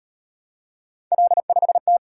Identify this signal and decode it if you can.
Morse code, heard as a keyed tone, about 36 wpm, 700 Hz. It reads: L5T